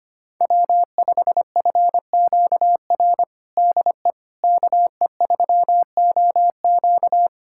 Morse code, heard as a keyed tone, about 25 wpm, 700 Hz. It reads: W5FQR DE KE3OQ
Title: USA Callsign DE Callsign - 5 Characters - 25wpm